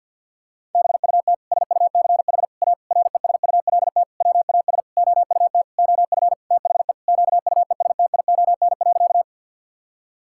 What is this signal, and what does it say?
Morse code, heard as a keyed tone, about 46 wpm, 700 Hz. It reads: BUT SUCH A RESULT WAS OUT OF THE QUESTION.
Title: Sentences from Top 1000 Words - Review Part 5 - 50wpm